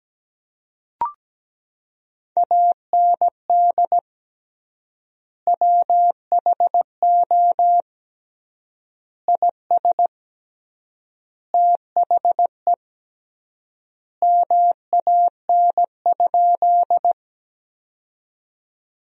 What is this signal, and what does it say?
Morse code, heard as a keyed tone, about 17 wpm, 700 Hz. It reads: AND WHO IS THE MAN?
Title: 3x Word-Spacing - Sentences from Top 200 Words - 17wpm